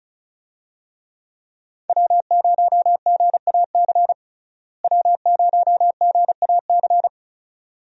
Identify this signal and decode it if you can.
Morse code, heard as a keyed tone, about 36 wpm, 700 Hz. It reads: W0GAC W0GAC
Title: USA Call Signs - 5 Characters - Repeated 2x - 35wpm